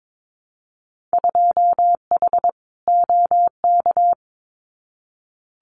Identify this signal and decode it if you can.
Morse code, heard as a keyed tone, about 22 wpm, 700 Hz. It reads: 2H OK
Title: Winter Field Day Exchanges Encore 22wpm